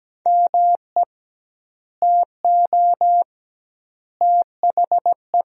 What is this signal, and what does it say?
Morse code, heard as a keyed tone, about 17 wpm, 700 Hz. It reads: ME TO THE
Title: 2x Word-Spacing - Sentences from Top 400 Words - 17wpm